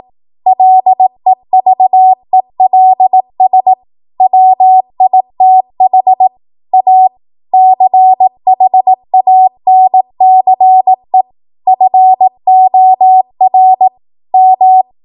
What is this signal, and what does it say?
Morse code, heard as a keyed tone, about 18 wpm, 750 Hz. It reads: LEVELS WITH A CHANCE FOR M
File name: Morse / 18WPM